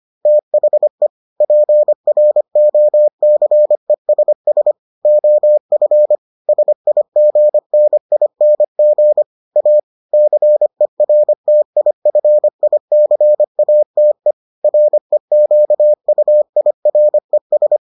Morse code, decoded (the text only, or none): THE PROCESS OF SIGNING A CERTIFICATE REQUIRES